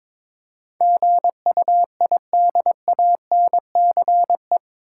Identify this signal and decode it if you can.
Morse code, heard as a keyed tone, about 22 wpm, 700 Hz. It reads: GUIDANCE
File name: practice / 22wpm / Eight Letter Words - 22wpm